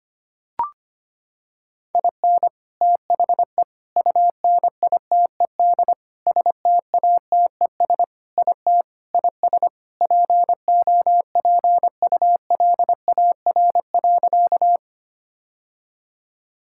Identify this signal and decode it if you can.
Morse code, heard as a keyed tone, about 25 wpm, 700 Hz. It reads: IN THE UNITED STATES IT IS POPULAR.
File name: practice / 25wpm / Sentences with 3-Word Phrases - 25wpm